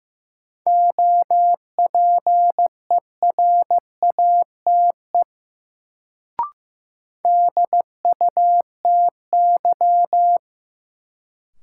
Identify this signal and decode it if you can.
Morse code, heard as a keyed tone, about 15 wpm, 700 Hz. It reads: OPERATE DUTY
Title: Sentences from Top 1500 Words - New Word Drill Alt - 15wpm